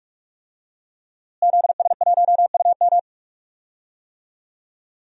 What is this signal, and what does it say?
Morse code, heard as a keyed tone, about 46 wpm, 700 Hz. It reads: ZS1UM
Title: DXCC-100 Call Signs - 5 Characters - Repeated 3x - 45wpm